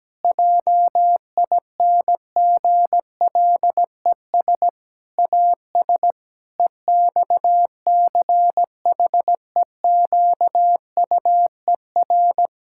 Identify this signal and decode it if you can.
Morse code, heard as a keyed tone, about 17 wpm, 700 Hz. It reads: JINGLES AS EXCHEQUER
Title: Pangram Challenge v2 - 17wpm